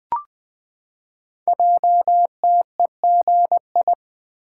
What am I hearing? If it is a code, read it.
Morse code, heard as a keyed tone, about 20 wpm, 700 Hz. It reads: JTEGI
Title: Code Groups - 5 Letters - 20wpm